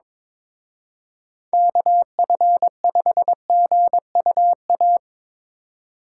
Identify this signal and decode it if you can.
Morse code, heard as a keyed tone, about 24 wpm, 700 Hz. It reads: KF5GUA
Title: Speed Racing - USA Call Signs 22wpm